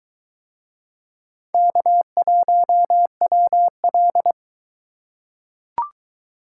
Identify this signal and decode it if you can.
Morse code, heard as a keyed tone, about 22 wpm, 700 Hz. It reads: K1WL